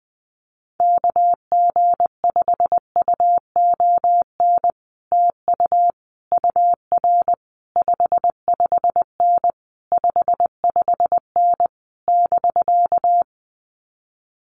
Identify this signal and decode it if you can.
Morse code, heard as a keyed tone, about 20 wpm, 700 Hz. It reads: KG5UON TU UR 55N 55N <BK>
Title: Common QSO Phrases Encore 20wpm